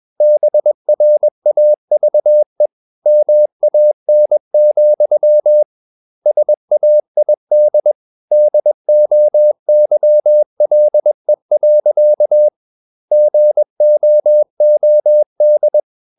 Morse code, heard as a keyed tone, about 21 wpm, 600 Hz. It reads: BRAVE MAN, SAID DOYLE. GOOD
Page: CW med Gnister